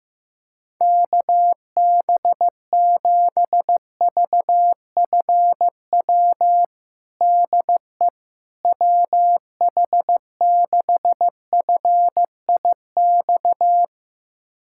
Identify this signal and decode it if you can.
Morse code, heard as a keyed tone, about 16 wpm, 700 Hz. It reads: KB7VFW DE WH6FIX
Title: USA Callsign DE Callsign - 6 Characters - 15wpm